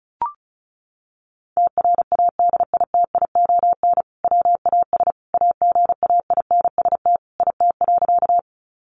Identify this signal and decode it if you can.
Morse code, heard as a keyed tone, about 34 wpm, 700 Hz. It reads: TRADITION WAS AGAINST IT.
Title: Sentences from Top 1500 Words 35wpm